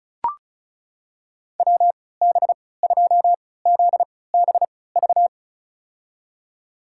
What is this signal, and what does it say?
Morse code, heard as a keyed tone, about 36 wpm, spaced as for 20 wpm, 700 Hz. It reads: WB2ZBV
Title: USA Call Signs - 6 Characters - 3x Character Spacing - 35wpm